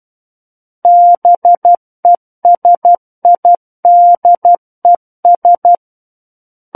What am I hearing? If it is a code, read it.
Morse code, heard as a keyed tone, about 12 wpm, 700 Hz. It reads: BESIDES
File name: Book One / 03-Chapter 3 - 12 WPM